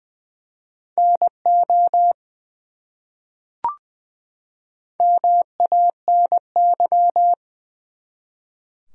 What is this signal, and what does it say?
Morse code, heard as a keyed tone, about 20 wpm, 700 Hz. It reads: NO MANY
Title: Top 100 Words 20wpm